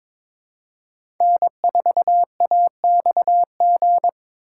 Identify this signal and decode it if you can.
Morse code, heard as a keyed tone, about 22 wpm, 690 Hz. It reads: N4AXG
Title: USA Call Signs - 5 Characters - 2x Character Spacing - 22wpm